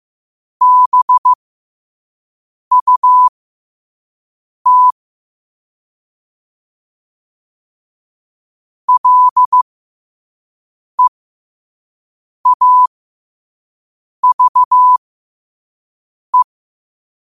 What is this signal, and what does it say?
Morse code, heard as a keyed tone, about 15 wpm, spaced as for 5 wpm, 1000 Hz. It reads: BUT LEAVE
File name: qotdfiles / 5wpm